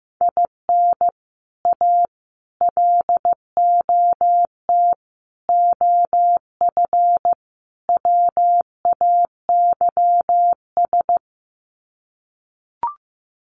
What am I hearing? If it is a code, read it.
Morse code, heard as a keyed tone, about 15 wpm, 700 Hz. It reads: IN A LOT OF WAYS
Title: Sets of 5 Words Encore 15wpm